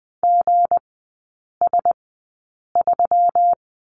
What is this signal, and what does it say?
Morse code, heard as a keyed tone, about 20 wpm, spaced as for 8 wpm, 690 Hz. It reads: GS3
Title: 198 Lesson 32e - Letter k - Three Characters 20wpm